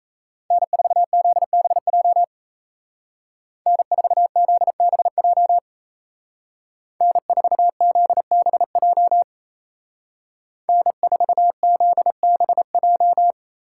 Morse code, heard as a keyed tone, about 34 wpm, 700 Hz. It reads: N4ZBJ N4ZBJ N4ZBJ N4ZBJ
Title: Speed Racing - USA 1x3 Call Signs 28wpm